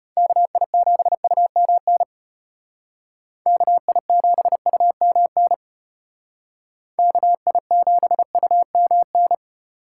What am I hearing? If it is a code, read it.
Morse code, heard as a keyed tone, about 33 wpm, 700 Hz. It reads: KI7UMN KI7UMN KI7UMN